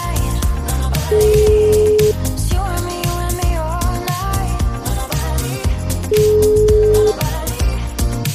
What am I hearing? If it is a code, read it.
Ringback tone (Europe, 425 Hz), with noise.